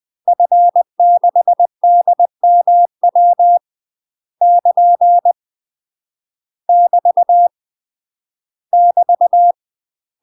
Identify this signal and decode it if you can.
Morse code, heard as a keyed tone, about 20 wpm, 700 Hz. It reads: F6DMW <KN> <BT> <BT>
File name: UK MASTER QSOs TRAINING / 20WPM / UK MASTER QSOs 0053